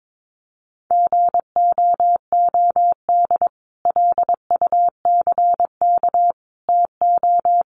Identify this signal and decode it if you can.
Morse code, heard as a keyed tone, about 22 wpm, 700 Hz. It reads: GOOD LUCK TO
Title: Sentences from Top 1500 Words 22wpm